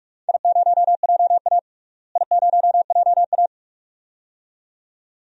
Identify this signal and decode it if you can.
Morse code, heard as a keyed tone, about 46 wpm, 700 Hz. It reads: I0JA I0JA
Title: DXCC-100 Call Signs - 4 Characters - Repeated 2x - 45wpm